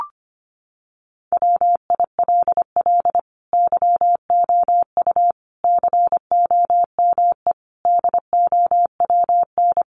Morse code, heard as a keyed tone, about 25 wpm, 700 Hz. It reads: WILL YOU COME DOWN
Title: Sentences from Top 1500 Words 25wpm